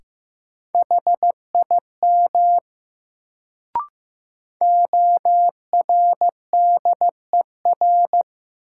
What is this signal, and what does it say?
Morse code, heard as a keyed tone, about 15 wpm, 700 Hz. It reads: HIM ORDER
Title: Sets of 2 Words - New Word Drill - 15wpm